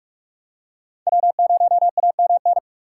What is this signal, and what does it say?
Morse code, heard as a keyed tone, about 44 wpm, 690 Hz. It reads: W0AMN
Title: USA Call Signs - 5 Characters - 6x Character Spacing - 45wpm